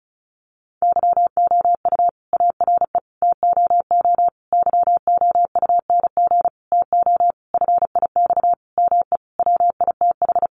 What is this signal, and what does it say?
Morse code, heard as a keyed tone, about 34 wpm, 700 Hz. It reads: YOU ARE TOO YOUNG TO FIX ME WITH
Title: Sentences from Top 1400 Words 35wpm